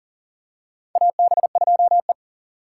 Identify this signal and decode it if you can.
Morse code, heard as a keyed tone, about 38 wpm, 700 Hz. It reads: AB2E
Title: USA Call Signs - 4 Characters - 6x Character Spacing - 40wpm